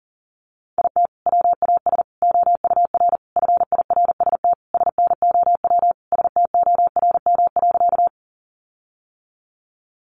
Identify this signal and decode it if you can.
Morse code, heard as a keyed tone, about 40 wpm, 700 Hz. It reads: IT WAS OUR FIRST SNOW STORM.
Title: Sentences from Top 1500 Words 40wpm